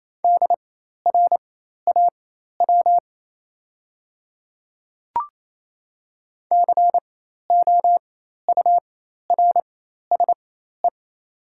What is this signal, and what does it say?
Morse code, heard as a keyed tone, about 28 wpm, spaced as for 13 wpm, 700 Hz. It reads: DRAW COURSE
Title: 4x Character Spacing - Top 500 Words - 28wpm